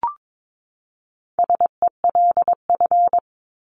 Morse code, heard as a keyed tone, about 22 wpm, 700 Hz. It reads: SELF
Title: Top 500 Words 22wpm